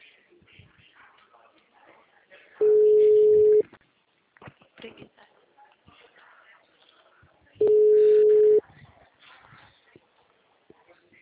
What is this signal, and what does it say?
Ringback tone (Europe, 425 Hz), with speech.